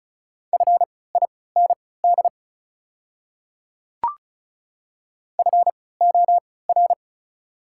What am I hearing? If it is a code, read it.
Morse code, heard as a keyed tone, about 36 wpm, spaced as for 20 wpm, 690 Hz. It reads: FIND FOR
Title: 3x Character Spacing - Top 100 Words - 35wpm